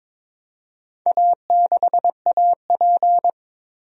Morse code, heard as a keyed tone, about 22 wpm, 700 Hz. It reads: A6AP
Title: DXCC-100 Call Signs - 4 Characters - 22wpm